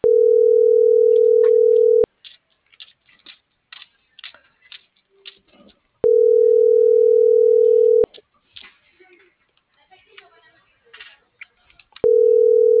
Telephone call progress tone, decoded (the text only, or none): ringback tone